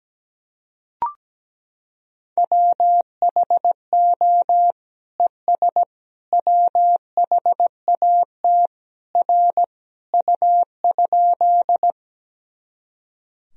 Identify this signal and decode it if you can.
Morse code, heard as a keyed tone, about 17 wpm, 700 Hz. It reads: WHO ES WHAT R U?
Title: Sentences from Top 100 Words and Abbreviations 17wpm